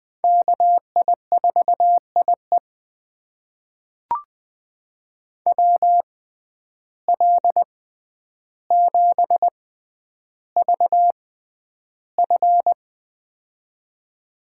Morse code, heard as a keyed tone, about 20 wpm, spaced as for 10 wpm, 700 Hz. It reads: KI4IE WL7VF
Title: USA Call Signs - 5 Characters - 6x Character Spacing - 20wpm